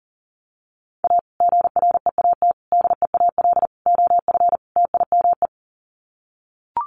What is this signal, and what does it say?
Morse code, heard as a keyed tone, about 40 wpm, 700 Hz. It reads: A GREAT DEAL OF TIME
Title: Sets of 5 Words Encore2 40wpm